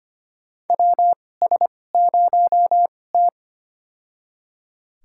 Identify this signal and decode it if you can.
Morse code, heard as a keyed tone, about 25 wpm, spaced as for 18 wpm, 700 Hz. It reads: WS0T
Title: USA Call Signs - 4 Characters - 2x Character Spacing - 25wpm